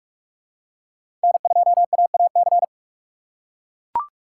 Morse code, heard as a keyed tone, about 41 wpm, 700 Hz. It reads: N2AAC
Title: USA Call Signs - 4x Character Spacing - 45wpm